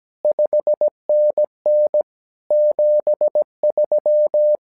Morse code, heard as a keyed tone, about 17 wpm, 600 Hz. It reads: 5NN 73
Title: SOTA QSO Practice 17wpm